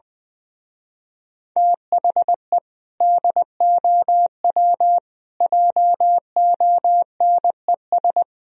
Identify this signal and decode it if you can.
Morse code, heard as a keyed tone, about 20 wpm, 700 Hz. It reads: THE DOW JONES
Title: Speed-Racing - Sets of 5 Words 17wpm